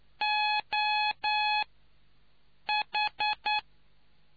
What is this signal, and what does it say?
Morse code, heard as a keyed tone, about 9 wpm, spaced as for 6 wpm, 840 Hz. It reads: OH